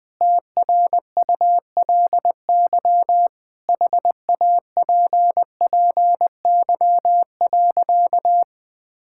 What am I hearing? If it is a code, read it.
Morse code, heard as a keyed tone, about 20 wpm, 700 Hz. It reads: TRULY HAPPY.
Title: Sentences with 7-Word Phrases - 20wpm